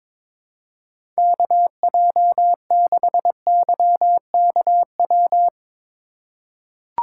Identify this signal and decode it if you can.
Morse code, heard as a keyed tone, about 22 wpm, 700 Hz. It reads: KJ6YKW
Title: USA Call Signs - 6 Characters - 4x Character Spacing - 22wpm